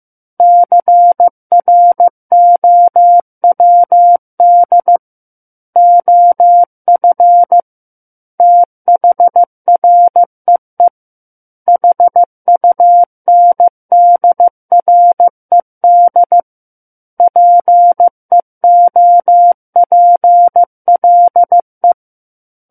Morse code, heard as a keyed tone, about 15 wpm, 700 Hz. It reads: CROWD OF THREE HUNDRED PEOPLE
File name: Book One / 06-Chapter 6 - 15 WPM